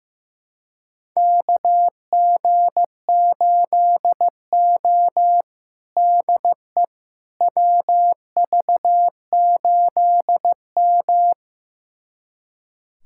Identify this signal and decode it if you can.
Morse code, heard as a keyed tone, about 15 wpm, 700 Hz. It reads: KG8O DE WV8M
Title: USA Callsign DE Callsign - 4 Characters - 15wpm